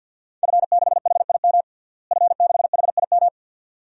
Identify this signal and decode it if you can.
Morse code, heard as a keyed tone, about 51 wpm, 700 Hz. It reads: F6HIM F6HIM
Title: DXCC-100 Call Signs - 5 Characters - Repeated 2x - 50wpm